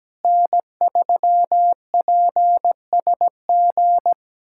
Morse code, heard as a keyed tone, about 17 wpm, 700 Hz. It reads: N3PSG
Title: USA Callsign DE Callsign - 5 Characters - 17wpm